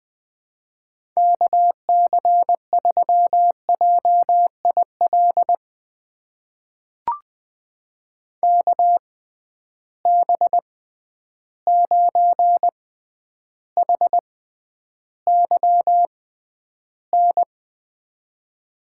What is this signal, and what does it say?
Morse code, heard as a keyed tone, about 20 wpm, spaced as for 10 wpm, 700 Hz. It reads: KC3JIL KB9HYN
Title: USA Call Signs - 6 Characters - 6x Character Spacing - 20wpm